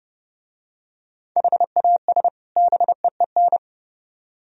Morse code, heard as a keyed tone, about 28 wpm, 700 Hz. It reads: HAS BEEN